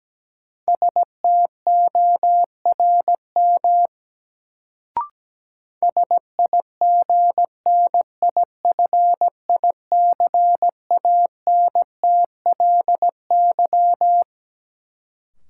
Morse code, heard as a keyed tone, about 17 wpm, 700 Hz. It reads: STORM SIGNIFICANTLY
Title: Sentences from Top 1500 Words - New Word Drill Alt - 17wpm